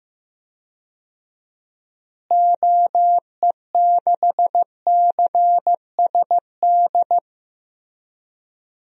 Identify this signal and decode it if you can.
Morse code, heard as a keyed tone, about 15 wpm, 700 Hz. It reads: OE6CSD
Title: DXCC-100 Call Signs - 6 Characters - Repeated 2x - 15wpm